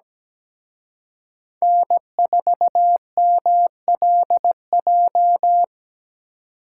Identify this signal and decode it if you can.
Morse code, heard as a keyed tone, about 20 wpm, 700 Hz. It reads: N4MLJ
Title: Speed Racing - USA 1x3 Call Signs 17wpm